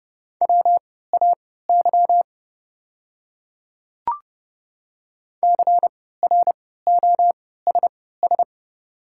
Morse code, heard as a keyed tone, about 30 wpm, spaced as for 17 wpm, 690 Hz. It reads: WAY CROSS
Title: 3x Character Spacing - Top 300 Words - 30wpm